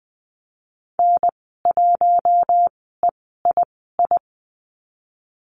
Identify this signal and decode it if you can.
Morse code, heard as a keyed tone, about 20 wpm, spaced as for 14 wpm, 700 Hz. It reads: N1EII
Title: 042 Lesson 8g - Number 1 - Call Signs 20wpm